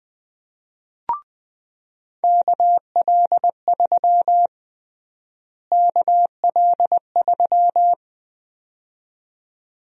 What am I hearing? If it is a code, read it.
Morse code, heard as a keyed tone, about 21 wpm, 700 Hz. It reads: KL3 KL3
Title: USA Call Sign Prefixes - Repeated 2x - 20wpm